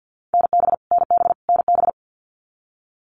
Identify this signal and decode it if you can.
Morse code, heard as a keyed tone, about 50 wpm, 700 Hz. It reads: NB NB NB
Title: Canadian Provinces and Territories - Repeated 3x - 50wpm